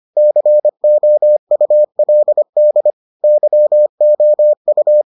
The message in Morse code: COULD YOU

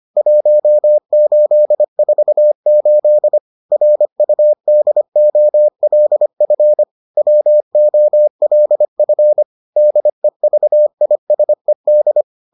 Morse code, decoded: 1848 RUDOLF WOLF DEVISED

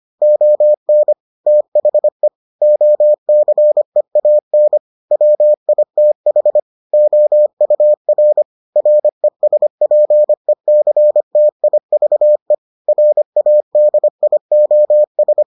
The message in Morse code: ON THE OCEAN WITH OUR RESPECTIVE RADIOS

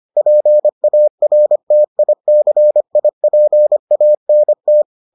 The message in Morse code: PARTICIPANT

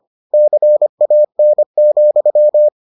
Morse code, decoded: CAN,

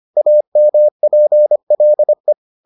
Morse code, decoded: AMPLE

